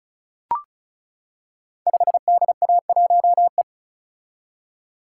Morse code, heard as a keyed tone, about 36 wpm, 700 Hz. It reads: 5DA1E
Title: Code Groups - 5 Letter:Numbers - 35wpm